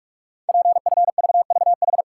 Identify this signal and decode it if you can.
Morse code, heard as a keyed tone, about 44 wpm, 690 Hz. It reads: PFVVH
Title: Code Groups - 5 Letters - 45wpm